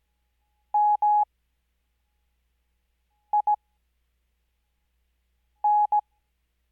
Morse code, heard as a keyed tone, about 17 wpm, spaced as for 4 wpm, 840 Hz. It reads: MIN
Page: Farnsworth Spacing: 15WPM Character speed Spaced at 5WPM.